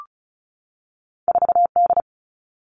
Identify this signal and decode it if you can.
Morse code, heard as a keyed tone, about 36 wpm, 700 Hz. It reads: 4D
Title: All Cut Numbers - 2-Digit 35wpm